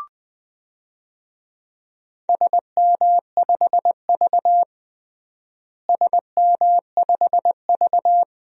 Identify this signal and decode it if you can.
Morse code, heard as a keyed tone, about 20 wpm, 700 Hz. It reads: SM5V SM5V
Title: DXCC-100 Call Signs - 4 Characters - Repeated 2x - 20wpm